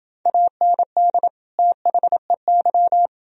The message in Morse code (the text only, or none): AND THEY